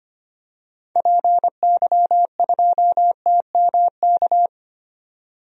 Morse code, heard as a keyed tone, about 25 wpm, 700 Hz. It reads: PY2TMK
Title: DXCC-100 Call Signs - 6 Characters - 25wpm